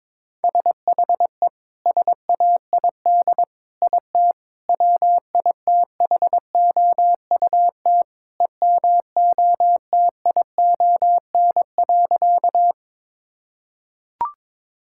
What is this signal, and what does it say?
Morse code, heard as a keyed tone, about 22 wpm, 700 Hz. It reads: SHE SAID IT WITHOUT EMOTION.
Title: Sentences from Top 1600 Words 22wpm